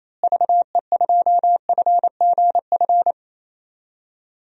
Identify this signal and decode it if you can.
Morse code, heard as a keyed tone, about 28 wpm, 700 Hz. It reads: VE2FGF
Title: DXCC-100 Call Signs - 6 Characters - Repeated 2x - 28wpm